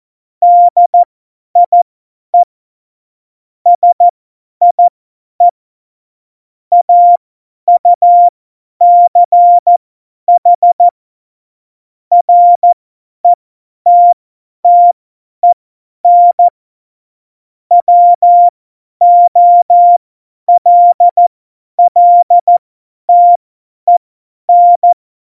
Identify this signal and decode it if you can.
Morse code, heard as a keyed tone, about 14 wpm, spaced as for 10 wpm, 700 Hz. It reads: DIE SIE AUCH RETTEN WOLLTEN
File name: Morsekurs / Witze / Witz 11 Juli 17, Tempo 50 F70